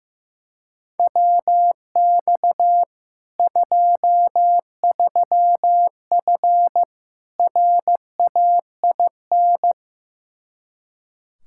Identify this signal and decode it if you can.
Morse code, heard as a keyed tone, about 15 wpm, 700 Hz. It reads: WX 23F RAIN
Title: Practice QSO - Exchange 2 - Info - 15wpm